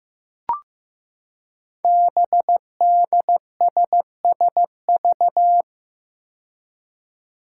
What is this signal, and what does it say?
Morse code, heard as a keyed tone, about 15 wpm, 700 Hz. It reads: BDSSV
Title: Five Challenging Letter-Number Combinations 15wpm